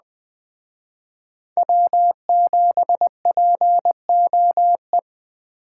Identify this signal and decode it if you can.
Morse code, heard as a keyed tone, about 20 wpm, 700 Hz. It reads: W7POE